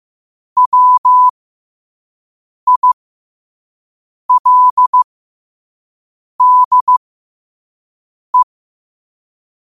Morse code, heard as a keyed tone, about 15 wpm, spaced as for 5 wpm, 1000 Hz. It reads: WILDE
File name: qotdfiles / 5wpm